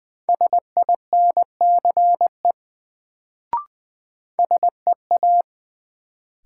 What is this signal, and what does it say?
Morse code, heard as a keyed tone, about 21 wpm, 700 Hz. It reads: SINCE SEA
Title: Sentences from Top 300 Words - New Word Drill - 20wpm